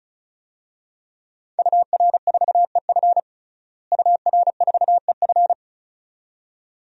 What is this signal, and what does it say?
Morse code, heard as a keyed tone, about 36 wpm, 700 Hz. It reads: UR4EF UR4EF
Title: DXCC-100 Call Signs - 5 Characters - Repeated 2x - 35wpm